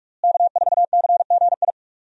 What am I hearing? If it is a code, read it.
Morse code, heard as a keyed tone, about 42 wpm, 700 Hz. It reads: K4CGI
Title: USA Call Signs - 5 Characters - Repeated 3x - 45wpm